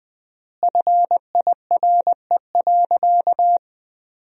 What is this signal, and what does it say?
Morse code, heard as a keyed tone, about 21 wpm, 700 Hz. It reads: FIRE.
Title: 3x Word-Spacing - Sentences from Top 400 Words - 20wpm